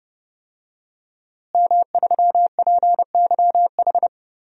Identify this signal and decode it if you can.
Morse code, heard as a keyed tone, about 29 wpm, 700 Hz. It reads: M3PYH
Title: DXCC-100 Call Signs - 5 Characters - Repeated 3x - 30wpm